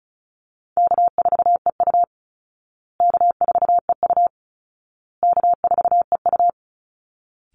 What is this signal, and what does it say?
Morse code, heard as a keyed tone, about 34 wpm, 700 Hz. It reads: K4EU K4EU K4EU
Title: Easy Call Signs Repeated 3x - 35wpm